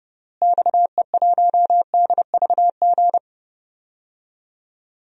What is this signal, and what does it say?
Morse code, heard as a keyed tone, about 30 wpm, 700 Hz. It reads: XE1DVG
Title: DXCC-100 Call Signs - 6 Characters - Repeated 3x - 30wpm